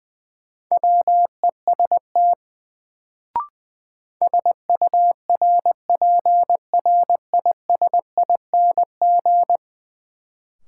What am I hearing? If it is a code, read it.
Morse code, heard as a keyed tone, about 20 wpm, 700 Hz. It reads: WEST SURPRISING
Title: Sets of 5 Words - New Word Drill - 20wpm